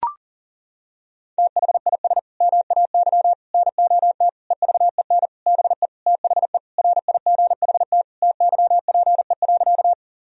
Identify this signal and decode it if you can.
Morse code, heard as a keyed tone, about 40 wpm, 700 Hz. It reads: THIS MAY NOT EVEN BE THE RIGHT TYPE.
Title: Sentences from Top 1000 Words - Review Part 3 - 40wpm